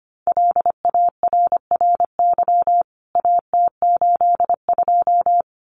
Morse code, heard as a keyed tone, about 25 wpm, 690 Hz. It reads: LARRY AT82
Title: CWT Exchange 25wpm